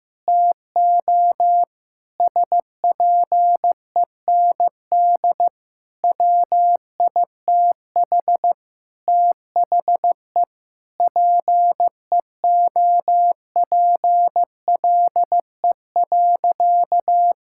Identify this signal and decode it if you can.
Morse code, heard as a keyed tone, about 15 wpm, 700 Hz. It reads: TO SPEND WITH THE PEOPLE.
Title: Sentences from Top 1000 Words - Review Part 5 - 15wpm